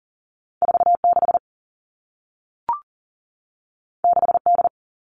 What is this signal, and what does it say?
Morse code, heard as a keyed tone, about 38 wpm, 700 Hz. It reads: 4 6 6 D
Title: Two Challenging Letter-Number Combinations 40wpm